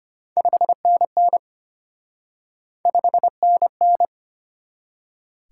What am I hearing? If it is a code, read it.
Morse code, heard as a keyed tone, about 33 wpm, 700 Hz. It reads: 5NN 5NN